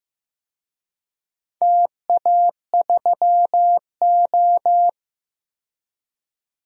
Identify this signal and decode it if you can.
Morse code, heard as a keyed tone, about 15 wpm, 700 Hz. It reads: TA3O
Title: DXCC-100 Call Signs - 4 Characters - 15wpm